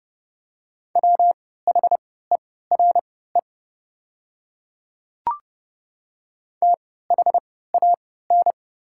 Morse code, heard as a keyed tone, about 29 wpm, spaced as for 17 wpm, 690 Hz. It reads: WHERE THAN
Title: 3x Character Spacing - Top 200 Words - 30wpm